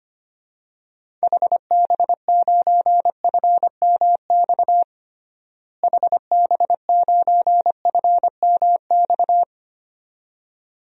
Morse code, heard as a keyed tone, about 25 wpm, 700 Hz. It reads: HB9FMX HB9FMX
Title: DXCC-100 Call Signs - 6 Characters - Repeated 2x - 25wpm